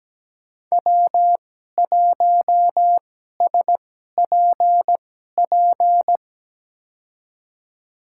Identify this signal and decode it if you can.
Morse code, heard as a keyed tone, about 17 wpm, spaced as for 12 wpm, 700 Hz. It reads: W1SPP
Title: USA Call Signs - 5 Characters - 2x Character Spacing - 17wpm